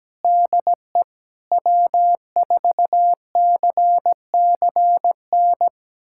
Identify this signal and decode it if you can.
Morse code, heard as a keyed tone, about 17 wpm, 700 Hz. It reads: DE W4CCN
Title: USA Callsign DE Callsign - 5 Characters - 17wpm